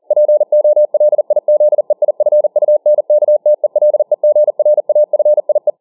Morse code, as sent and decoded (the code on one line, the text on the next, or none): .--. --- .-.. .. --.. .. ..-. ..- -. -.- - . .-.. --. .-. .- ..-. .. .
POLIZEIFUNKTELEGRAFIE